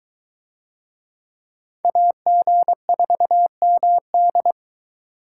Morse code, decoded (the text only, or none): AG4MD